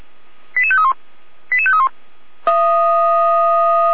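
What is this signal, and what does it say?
Five-tone selective call (ZVEI), heard as quick runs of five tones, tones of about 70 ms. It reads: 88531 88531
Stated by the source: ZVEI five-tone call received by SDR.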